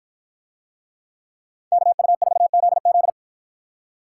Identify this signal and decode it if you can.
Morse code, heard as a keyed tone, about 49 wpm, 700 Hz. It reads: KU4ZZ